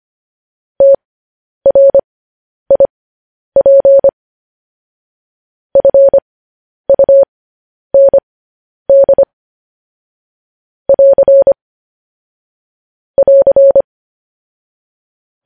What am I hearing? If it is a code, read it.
Morse code, heard as a keyed tone, about 25 wpm, spaced as for 10 wpm, 550 Hz. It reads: TRIP FUND <AR> <AR>
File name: podcast / News headlines-1x-25x10